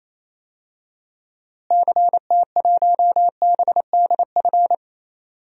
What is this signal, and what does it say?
Morse code, heard as a keyed tone, about 28 wpm, 700 Hz. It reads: CT1BDF
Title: DXCC-100 Call Signs - 6 Characters - Repeated 3x - 28wpm